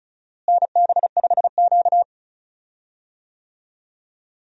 Morse code, heard as a keyed tone, about 36 wpm, 700 Hz. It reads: NB5Q
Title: USA Call Signs - 4 Characters - Repeated 3x - 35wpm